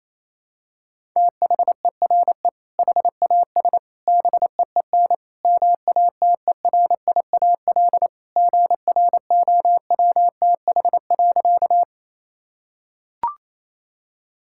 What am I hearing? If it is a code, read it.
Morse code, heard as a keyed tone, about 28 wpm, 700 Hz. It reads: THERE HAS BEEN MATERIAL GROWTH.
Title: Sentences from Top 1000 Words - Review Part 5 - 28wpm